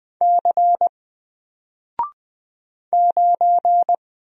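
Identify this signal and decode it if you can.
Morse code, heard as a keyed tone, about 20 wpm, 700 Hz. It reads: C 9
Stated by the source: Single letter and number Morse drill at 20 wpm, from N4LCD.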